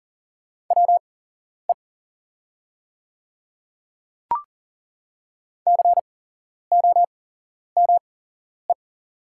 Morse code, heard as a keyed tone, about 39 wpm, spaced as for 11 wpm, 700 Hz. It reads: WE COME
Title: 8x Character Spacing - Top 100 Words - 40wpm